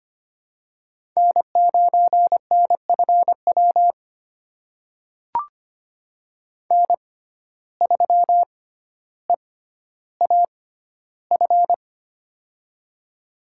Morse code, heard as a keyed tone, about 25 wpm, spaced as for 13 wpm, 700 Hz. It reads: N9NFW N3EAF
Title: USA Call Signs - 5 Characters - 6x Character Spacing - 25wpm